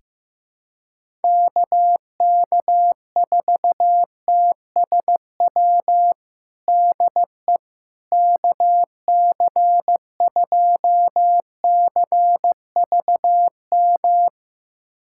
Morse code, heard as a keyed tone, about 15 wpm, 700 Hz. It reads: KK4TSW DE KC2CVM
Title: USA Callsign DE Callsign - 6 Characters - 15wpm